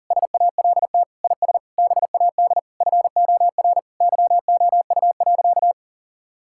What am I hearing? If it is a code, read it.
Morse code, heard as a keyed tone, about 40 wpm, 700 Hz. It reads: SALT IS BAD FOR YOU.
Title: Sentences from Top 1000 Words - Review Part 5 - 40wpm